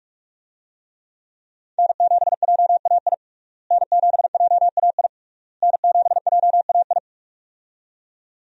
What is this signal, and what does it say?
Morse code, heard as a keyed tone, about 41 wpm, 700 Hz. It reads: N7JAI N7JAI N7JAI
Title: USA Call Signs - 5 Characters - Repeated 3x - 45wpm